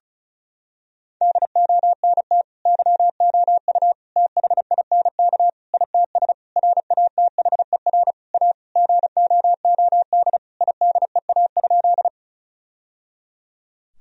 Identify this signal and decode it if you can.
Morse code, heard as a keyed tone, about 34 wpm, 700 Hz. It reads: DONT YOU THINK ITS RATHER A GOOD IDEA?
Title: Sentences from Top 1000 Words - Review Part 4 - 35wpm